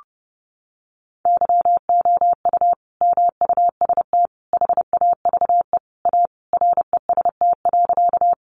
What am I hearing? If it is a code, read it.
Morse code, heard as a keyed tone, about 30 wpm, 700 Hz. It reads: YOU MUST HAVE A REST.
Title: Sentences from Top 500 Words 30wpm